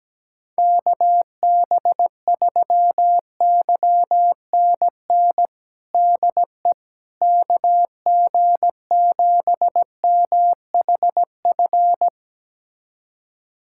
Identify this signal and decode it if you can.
Morse code, heard as a keyed tone, about 17 wpm, 700 Hz. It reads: KB3YNN DE KG7MHF
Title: USA Callsign DE Callsign - 6 Characters - 17wpm